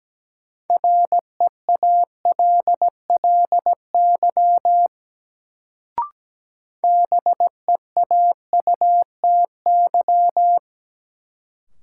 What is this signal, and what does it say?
Morse code, heard as a keyed tone, about 17 wpm, 700 Hz. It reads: REALLY BEAUTY
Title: Sentences from Top 600 Words - New Word Drill - 17wpm